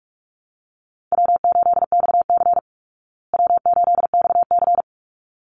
Morse code, heard as a keyed tone, about 44 wpm, 700 Hz. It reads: W8XC W8XC
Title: Easy Call Signs Repeated 2x - 45wpm